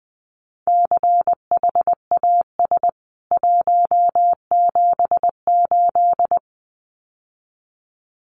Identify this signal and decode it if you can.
Morse code, heard as a keyed tone, about 20 wpm, 700 Hz. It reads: CHAS 178
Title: CWT Exchange 20wpm